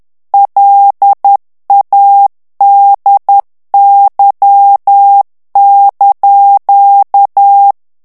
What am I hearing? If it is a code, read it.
Morse code, heard as a keyed tone, about 10 wpm, 800 Hz. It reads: LADY)
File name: morse mp3 / ITALIEN texte clair 1 vitesse 10 mots minute